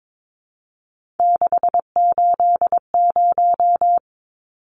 Morse code, heard as a keyed tone, about 22 wpm, 700 Hz. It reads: 680
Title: Numbers - 3-Digit 22wpm